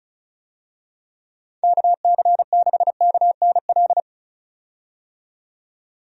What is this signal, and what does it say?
Morse code, heard as a keyed tone, about 34 wpm, 700 Hz. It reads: KC6KNL
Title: USA Call Signs - 6 Characters - 35wpm